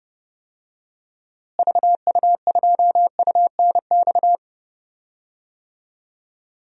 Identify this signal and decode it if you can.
Morse code, heard as a keyed tone, about 30 wpm, 700 Hz. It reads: VU2UNX
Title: DXCC-100 Call Signs - 6 Characters - Repeated 3x - 30wpm